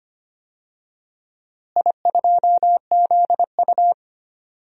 Morse code, heard as a keyed tone, about 25 wpm, 700 Hz. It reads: I2ZU
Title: DXCC-100 Call Signs - 4 Characters - Repeated 3x - 25wpm